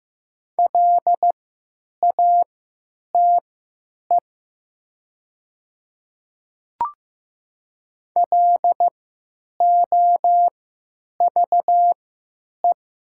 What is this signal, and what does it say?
Morse code, heard as a keyed tone, about 15 wpm, spaced as for 9 wpm, 700 Hz. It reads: LATE LOVE
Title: 3x Character Spacing - Top 500 Words - 15wpm